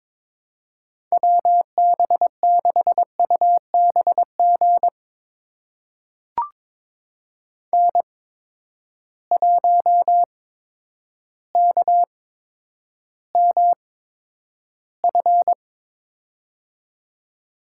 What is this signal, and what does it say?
Morse code, heard as a keyed tone, about 22 wpm, spaced as for 10 wpm, 700 Hz. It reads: WB6UBG N1KMF
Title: USA Call Signs - 8x Character Spacing - 22wpm